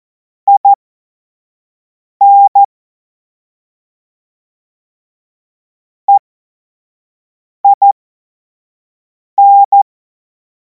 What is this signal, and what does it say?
Morse code, heard as a keyed tone, about 14 wpm, spaced as for 5 wpm, 800 Hz. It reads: IN EIN
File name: Morsekurs / Witze / Witz 12. Mai 2026 25 F70